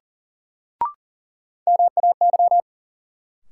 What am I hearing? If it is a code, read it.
Morse code, heard as a keyed tone, about 40 wpm, 700 Hz. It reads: MAY
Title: Sets of 2 Words - New Word Drill - 40wpm